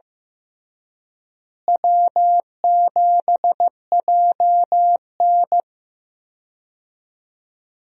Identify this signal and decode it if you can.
Morse code, heard as a keyed tone, about 16 wpm, 700 Hz. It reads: W7JN
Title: USA Call Signs - 4 Characters - Repeated 3x - 15wpm